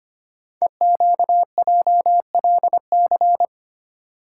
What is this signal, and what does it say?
Morse code, heard as a keyed tone, about 25 wpm, 700 Hz. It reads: EQJLC
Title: Code Groups - 5 Letter:Numbers - 25wpm